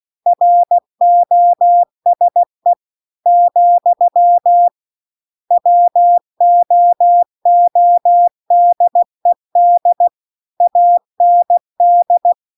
Morse code, heard as a keyed tone, about 16 wpm, 700 Hz. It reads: ROSE , WOODED AND